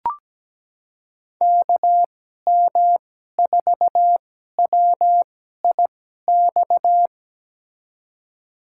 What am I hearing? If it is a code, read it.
Morse code, heard as a keyed tone, about 17 wpm, spaced as for 12 wpm, 700 Hz. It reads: KM4WIX
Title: USA Call Signs - 6 Characters - 2x Character Spacing - 17wpm